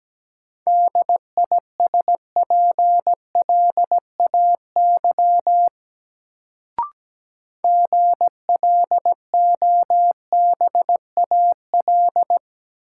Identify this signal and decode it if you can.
Morse code, heard as a keyed tone, about 17 wpm, 700 Hz. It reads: DISPLAY GLOBAL
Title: Sentences from Top 1200 Words - New Word Drill - 17wpm